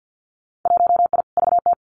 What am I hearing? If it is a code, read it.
Morse code, heard as a keyed tone, about 50 wpm, 700 Hz. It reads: 1I VT
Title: Winter Field Day Exchanges Encore 50wpm